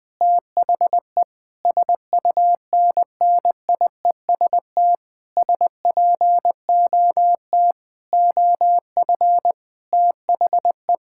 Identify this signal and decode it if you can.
Morse code, heard as a keyed tone, about 21 wpm, 700 Hz. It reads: THE SUNNIEST SPOT OF THE
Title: Difficult Sentences with Hints - 20wpm